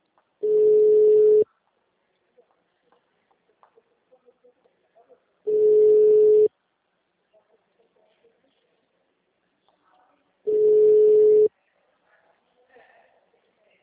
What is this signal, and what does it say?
Ringback tone (Europe, 425 Hz), with speech.
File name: out- -2017- - - .9628